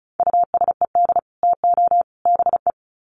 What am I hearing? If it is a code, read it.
Morse code, heard as a keyed tone, about 34 wpm, 700 Hz. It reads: USED TO BE
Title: Sets of 3 Words 35wpm